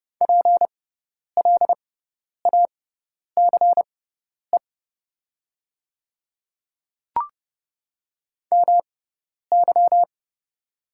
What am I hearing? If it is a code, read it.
Morse code, heard as a keyed tone, about 30 wpm, spaced as for 10 wpm, 690 Hz. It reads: PLACE MY
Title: 6x Character Spacing - Top 500 Words - 30wpm